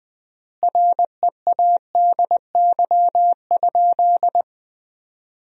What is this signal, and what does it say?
Morse code, heard as a keyed tone, about 20 wpm, 700 Hz. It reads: READY?
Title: 3x Word-Spacing - Sentences from Top 400 Words - 20wpm